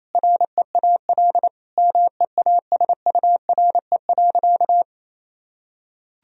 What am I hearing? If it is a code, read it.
Morse code, heard as a keyed tone, about 28 wpm, 700 Hz. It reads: REAL MEASURE.
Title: Sentences from Top 1000 Words - Review Part 1 - 28wpm